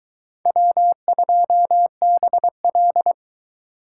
Morse code, heard as a keyed tone, about 22 wpm, 700 Hz. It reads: W2BL